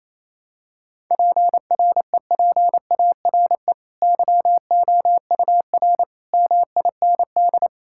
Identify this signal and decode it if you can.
Morse code, heard as a keyed tone, about 28 wpm, 700 Hz. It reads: PREPARE YOUR MIND
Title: Sentences from Top 1000 Words - Review Part 6 - 28wpm